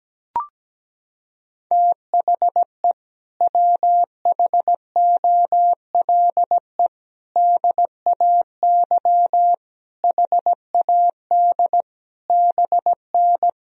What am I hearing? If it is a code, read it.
Morse code, heard as a keyed tone, about 17 wpm, 700 Hz. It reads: THE WHOLE DAY HAD BN
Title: Sentences from Top 500 Words and Abbreviations 17wpm